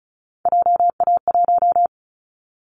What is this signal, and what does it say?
Morse code, heard as a keyed tone, about 34 wpm, 700 Hz. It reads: JA1
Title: Callsign Prefixes for Top 25 DXCC - 35wpm